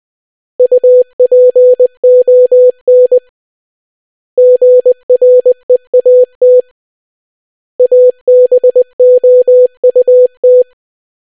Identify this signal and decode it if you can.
Morse code, heard as a keyed tone, about 20 wpm, 500 Hz. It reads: UPON GREAT ABOUT